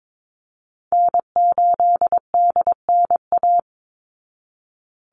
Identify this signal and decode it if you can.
Morse code, heard as a keyed tone, about 22 wpm, 700 Hz. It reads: N8DNA
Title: Call Signs Repeated 3x - 22wpm